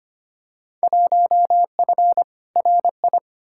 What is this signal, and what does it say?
Morse code, heard as a keyed tone, about 25 wpm, 700 Hz. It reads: 1F RI
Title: Field Day Exchanges Encore - 25wpm